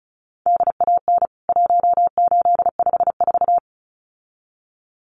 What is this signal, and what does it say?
Morse code, heard as a keyed tone, about 34 wpm, 700 Hz. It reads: DAN 1854
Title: CWT Exchange 35wpm